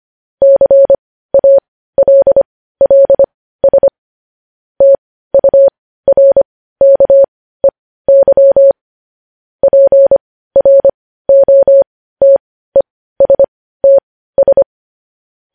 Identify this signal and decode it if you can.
Morse code, heard as a keyed tone, about 25 wpm, spaced as for 15 wpm, 550 Hz. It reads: CALLS TURKEY PROTESTS
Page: News Headlines in Morse Code at 15 WPM